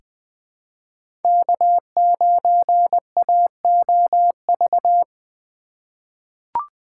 Morse code, heard as a keyed tone, about 20 wpm, 700 Hz. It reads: K9AOV
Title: USA Call Signs - 5 Characters - 2x Character Spacing - 20wpm